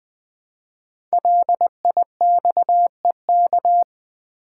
Morse code, heard as a keyed tone, about 21 wpm, 700 Hz. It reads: LIXEK
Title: Code Groups - 5 Letter:Numbers - 20wpm